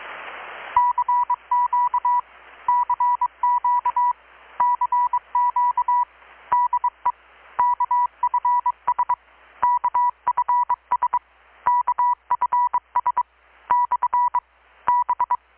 Morse code, heard as a keyed tone, about 23 wpm, 980 Hz, with radio noise. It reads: CQ CQ CQ DE KFS KFS KFS / B
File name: blog / swl / 6-abril-2019-12695-USB-KPS